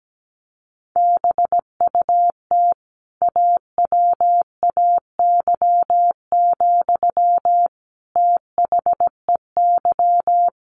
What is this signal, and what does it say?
Morse code, heard as a keyed tone, about 17 wpm, 700 Hz. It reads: BUT AWAY, THEY
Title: Sentences from Top 1400 Words 17wpm